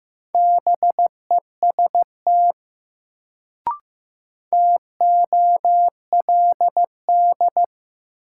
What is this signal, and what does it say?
Morse code, heard as a keyed tone, about 15 wpm, 700 Hz. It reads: BEST TOLD
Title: Sentences from Top 400 Words - New Word Drill - 15wpm